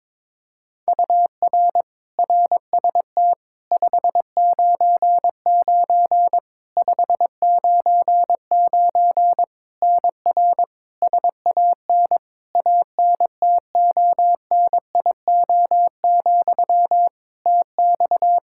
Morse code, heard as a keyed tone, about 22 wpm, 700 Hz. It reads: UR RST 599 599 NR SAN ANTONIO, TX
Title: K1USN SST QSO Practice 22wpm